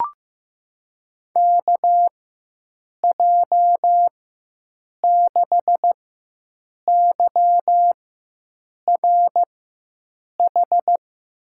Morse code, heard as a keyed tone, about 15 wpm, spaced as for 7 wpm, 700 Hz. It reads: KJ6YRH
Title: USA Call Signs - 6 Characters - 4x Character Spacing - 15wpm